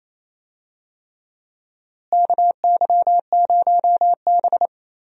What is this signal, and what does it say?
Morse code, heard as a keyed tone, about 27 wpm, 700 Hz. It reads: KY0B
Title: USA Call Signs - 4 Characters - Repeated 3x - 28wpm